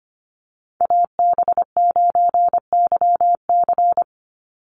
Morse code, heard as a keyed tone, about 25 wpm, 700 Hz. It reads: AB9YC
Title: Call Signs Repeated 2x - 25wpm